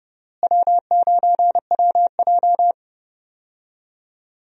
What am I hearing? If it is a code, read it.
Morse code, heard as a keyed tone, about 32 wpm, 700 Hz. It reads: W9WJ
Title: USA Call Signs - 4 Characters - Repeated 3x - 30wpm